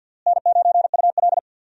Morse code, heard as a keyed tone, about 53 wpm, 700 Hz. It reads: N9UL
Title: USA Call Signs - 4 Characters - Repeated 3x - 50wpm